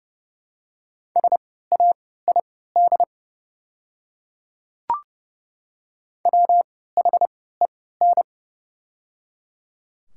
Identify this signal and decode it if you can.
Morse code, heard as a keyed tone, about 30 wpm, spaced as for 17 wpm, 700 Hz. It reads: SAID WHEN
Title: 3x Character Spacing - Top 100 Words - 30wpm